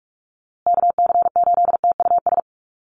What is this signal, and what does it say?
Morse code, heard as a keyed tone, about 44 wpm, 700 Hz. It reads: KC8TUS
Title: USA Call Signs Repeated 2x 45wpm